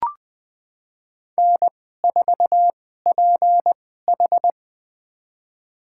Morse code, heard as a keyed tone, about 20 wpm, spaced as for 15 wpm, 700 Hz. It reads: N4PH
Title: USA Call Signs - 4 Characters - 2x Character Spacing - 20wpm